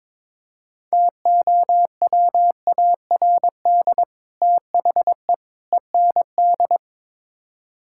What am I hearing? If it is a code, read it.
Morse code, heard as a keyed tone, about 22 wpm, 700 Hz. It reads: TOWARD THE END
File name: practice / 22wpm / Build A Phrase - 5-Word Phrases - 22wpm